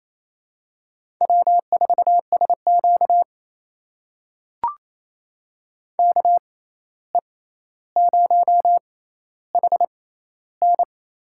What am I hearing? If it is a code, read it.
Morse code, heard as a keyed tone, about 28 wpm, spaced as for 13 wpm, 700 Hz. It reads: W4SQ KE0HN
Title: USA Call Signs - 6x Character Spacing - 28wpm